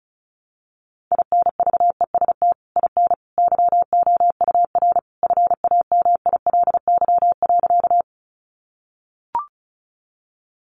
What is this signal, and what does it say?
Morse code, heard as a keyed tone, about 34 wpm, 700 Hz. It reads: INVEST IN YOUR FAMILY.
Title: Sentences from Top 1100 Words 35wpm